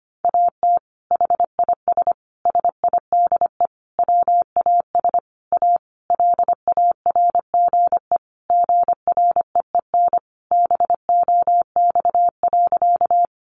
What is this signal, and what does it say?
Morse code, heard as a keyed tone, about 25 wpm, 700 Hz. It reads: AT HIS SIDE WAS A LARGE GREEN BOX.
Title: Sentences from Top 500 Words 25wpm